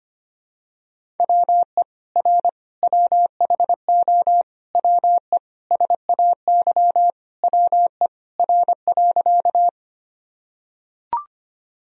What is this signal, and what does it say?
Morse code, heard as a keyed tone, about 25 wpm, 700 Hz. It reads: WE R WHO WE SAY WE R.
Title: Sentences from Top 200 Words and Abbreviations 25wpm